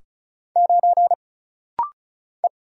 Morse code, heard as a keyed tone, about 36 wpm, 700 Hz. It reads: 9 E
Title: Single Letter-Number - Rapid-Fire 35wpm